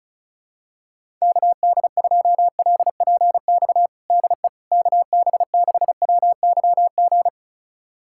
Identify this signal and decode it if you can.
Morse code, heard as a keyed tone, about 36 wpm, 700 Hz. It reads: KD2LPX DE KB6WYG
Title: USA Callsign DE Callsign - 6 Characters - 35wpm